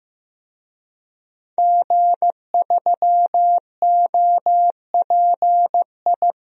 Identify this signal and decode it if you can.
Morse code, heard as a keyed tone, about 15 wpm, 700 Hz. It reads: G3OPI
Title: DXCC-100 Call Signs - 5 Characters - 15wpm